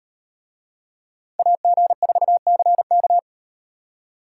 Morse code, heard as a keyed tone, about 38 wpm, 700 Hz. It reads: AG4CK